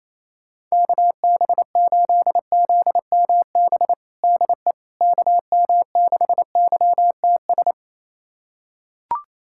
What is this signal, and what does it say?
Morse code, heard as a keyed tone, about 28 wpm, 700 Hz. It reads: KB8ZMB DE KM6YTS
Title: USA Callsign DE Callsign - 6 Characters - 28wpm